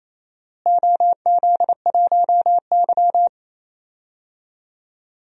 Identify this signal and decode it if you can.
Morse code, heard as a keyed tone, about 28 wpm, 700 Hz. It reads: OZ1Y
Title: DXCC-100 Call Signs - 4 Characters - Repeated 2x - 28wpm